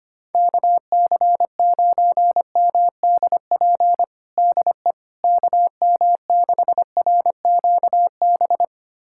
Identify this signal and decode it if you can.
Morse code, heard as a keyed tone, about 25 wpm, 700 Hz. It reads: KC9MDP DE KM6RQB
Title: USA Callsign DE Callsign - 6 Characters - 25wpm